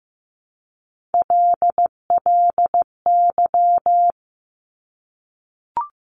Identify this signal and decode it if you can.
Morse code, heard as a keyed tone, about 15 wpm, 700 Hz. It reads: LLY
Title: Top Three Letter Combinations in Words 15wpm